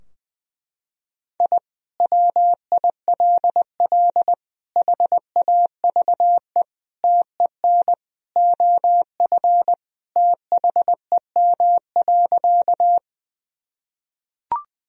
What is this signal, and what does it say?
Morse code, heard as a keyed tone, about 20 wpm, 700 Hz. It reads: I WILL HAVE TEN OF THEM.
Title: Sentences from Top 1000 Words - Review Part 2 - 20wpm